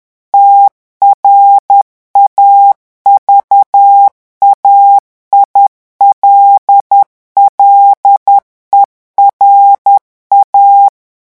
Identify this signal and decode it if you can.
Morse code, heard as a keyed tone, about 10 wpm, 800 Hz. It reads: TRAVAILLERA
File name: revision9 citations humoristiques vitesse 10 mots minute